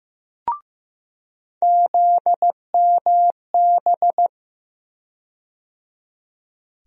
Morse code, heard as a keyed tone, about 15 wpm, 700 Hz. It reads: ZMB
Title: USA Call Sign Postfixes - 3 Letters 15wpm